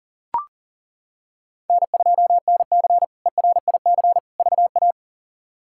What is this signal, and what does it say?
Morse code, heard as a keyed tone, about 37 wpm, 700 Hz. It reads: N2NC ERIC VA
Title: K1USN SST QSO Practice - 40wpm